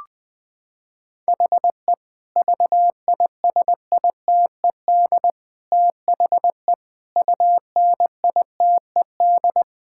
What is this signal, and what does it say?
Morse code, heard as a keyed tone, about 20 wpm, 700 Hz. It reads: HE VISITED THE UNITED
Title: Sentences with 4-Word Phrases - 20wpm